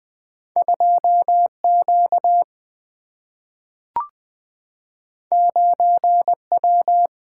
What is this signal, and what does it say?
Morse code, heard as a keyed tone, about 20 wpm, 700 Hz. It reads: 2 Q 9 W
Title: Single-Letter Single-Number 20wpm